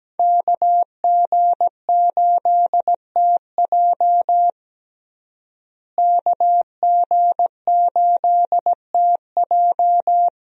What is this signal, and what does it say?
Morse code, heard as a keyed tone, about 17 wpm, 700 Hz. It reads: KG8TJ KG8TJ
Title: USA Call Signs - 5 Characters - Repeated 2x - 17wpm